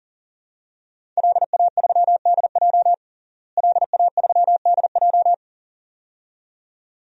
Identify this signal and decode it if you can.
Morse code, heard as a keyed tone, about 37 wpm, 700 Hz. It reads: LA3DJ LA3DJ
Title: DXCC-100 Call Signs - 5 Characters - Repeated 2x - 40wpm